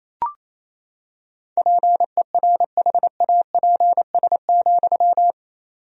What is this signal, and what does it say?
Morse code, heard as a keyed tone, about 28 wpm, 700 Hz. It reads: PERHAPS,
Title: 4x Word-Spacing - Sentences from Top 500 Words - 28wpm